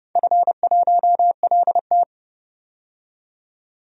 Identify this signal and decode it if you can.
Morse code, heard as a keyed tone, about 30 wpm, 700 Hz. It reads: F1LT
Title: DXCC-100 Call Signs - 4 Characters - 30wpm